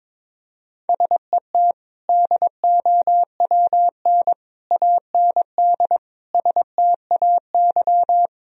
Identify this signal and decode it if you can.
Morse code, heard as a keyed tone, about 21 wpm, 700 Hz. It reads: SET DOWN AND STAY
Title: Sentences from Top 1000 Words - Review Part 1 - 22wpm